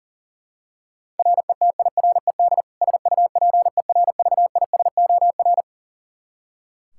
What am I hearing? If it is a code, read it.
Morse code, heard as a keyed tone, about 40 wpm, 700 Hz. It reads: RETIRED SUPERVISOR
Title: Practice QSO - Exchange 3 - More Info - 40wpm